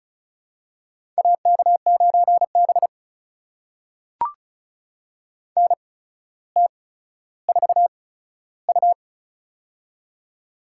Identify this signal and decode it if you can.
Morse code, heard as a keyed tone, about 36 wpm, spaced as for 15 wpm, 700 Hz. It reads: AK9B NT4U
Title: USA Call Signs - 4 Characters - 8x Character Spacing - 35wpm